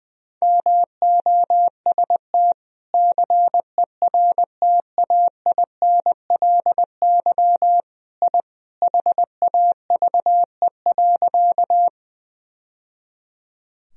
Morse code, heard as a keyed tone, about 21 wpm, 700 Hz. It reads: MOST CERTAINLY I HAVE.
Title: Sentences from Top 1000 Words - Review Part 1 - 20wpm